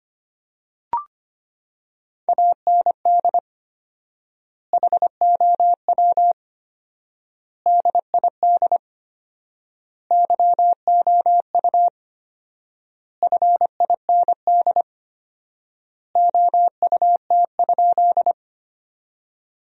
Morse code, heard as a keyed tone, about 25 wpm, 700 Hz. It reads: AND HOW DID YOU FIND OUT?
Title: 4x Word-Spacing - Sentences from Top 200 Words - 25wpm